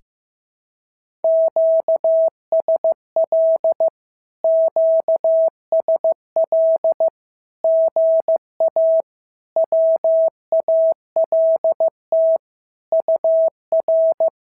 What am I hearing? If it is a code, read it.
Morse code, heard as a keyed tone, about 15 wpm, 650 Hz. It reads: QSL QSL GA WALT UR
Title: SOTA QSO Practice 15wpm